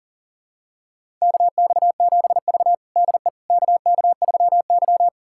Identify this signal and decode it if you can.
Morse code, heard as a keyed tone, about 40 wpm, 700 Hz. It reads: KX7V DE KK3Y
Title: USA Callsign DE Callsign - 4 Characters - 40wpm